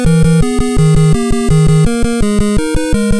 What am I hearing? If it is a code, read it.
There is no code.